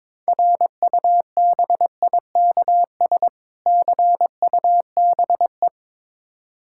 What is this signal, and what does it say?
Morse code, heard as a keyed tone, about 21 wpm, 700 Hz. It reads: RUBIKS CUBE
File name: Difficult Words with Hints - 22wpm